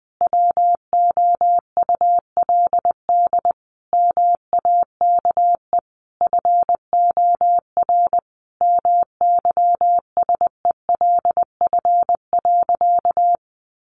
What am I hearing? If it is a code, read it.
Morse code, heard as a keyed tone, about 20 wpm, 700 Hz. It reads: WOULD MAKE FOR MYSELF.
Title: Sentences from Top 1300 Words 20wpm